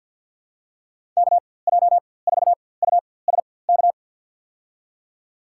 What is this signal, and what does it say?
Morse code, heard as a keyed tone, about 46 wpm, spaced as for 23 wpm, 700 Hz. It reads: KJ4USK
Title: USA Call Signs - 6 Characters - 4x Character Spacing - 50wpm